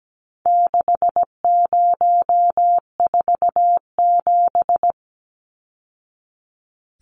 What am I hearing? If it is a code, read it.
Morse code, heard as a keyed tone, about 17 wpm, 700 Hz. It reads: 6047
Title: Numbers - 4-Digit 17wpm